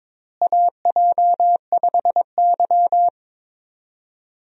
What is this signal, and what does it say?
Morse code, heard as a keyed tone, about 22 wpm, 700 Hz. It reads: AJ5Y
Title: USA Call Signs - 4 Characters - Repeated 3x - 22wpm